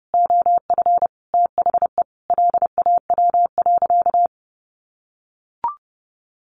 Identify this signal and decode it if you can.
Morse code, heard as a keyed tone, about 30 wpm, 700 Hz. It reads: OF THE LAW.
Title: Sentences from Top 1400 Words 30wpm